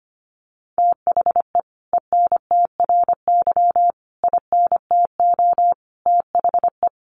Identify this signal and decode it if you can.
Morse code, heard as a keyed tone, about 25 wpm, 700 Hz. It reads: THE ENTRY INTO THE
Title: Sentences from Top 1500 Words 25wpm